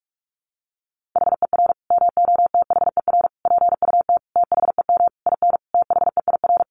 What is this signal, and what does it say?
Morse code, heard as a keyed tone, about 44 wpm, 700 Hz. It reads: HER MOTHER PUT THEM IN THEIR
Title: Sentences from Top 800 Words 45wpm